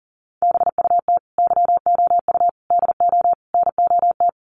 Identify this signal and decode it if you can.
Morse code, heard as a keyed tone, about 40 wpm, 700 Hz. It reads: BUT YOU DO NOT
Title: Sets of 4 Words 40wpm